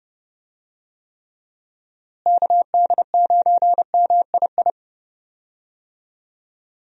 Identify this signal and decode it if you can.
Morse code, heard as a keyed tone, about 32 wpm, 700 Hz. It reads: KD9MII
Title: USA Call Signs - 6 Characters - Repeated 2x - 30wpm